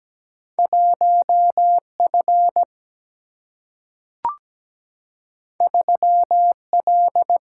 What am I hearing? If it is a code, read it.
Morse code, heard as a keyed tone, about 17 wpm, 700 Hz. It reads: 1F 3L
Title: Single-Letter Single-Number 17wpm